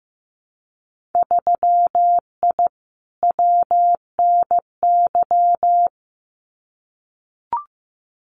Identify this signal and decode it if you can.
Morse code, heard as a keyed tone, about 15 wpm, 700 Hz. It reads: 3I WNY
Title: Winter Field Day Exchanges 15wpm